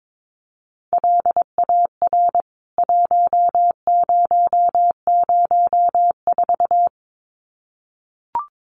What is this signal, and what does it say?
Morse code, heard as a keyed tone, about 22 wpm, 700 Hz. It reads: LAR 1004
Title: CWT Exchange 22wpm